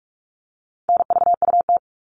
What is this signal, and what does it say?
Morse code, heard as a keyed tone, about 44 wpm, 700 Hz. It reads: NVUT
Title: All Cut Numbers - 4-Digit 45wpm